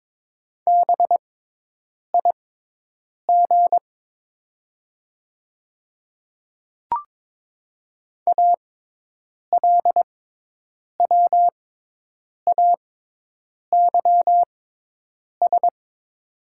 Morse code, heard as a keyed tone, about 22 wpm, spaced as for 8 wpm, 700 Hz. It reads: BIG ALWAYS
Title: 6x Character Spacing - Top 300 Words - 22wpm